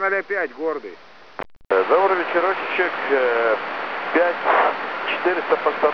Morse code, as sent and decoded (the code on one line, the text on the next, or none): none